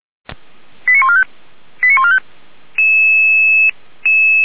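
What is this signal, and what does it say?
Five-tone selective call (ZVEI), heard as quick runs of five tones, tones of about 80 ms. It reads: 89145 89145
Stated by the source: ZVEI five-tone selective call 89135 received on SDR.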